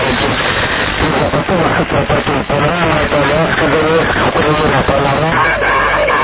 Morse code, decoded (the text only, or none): none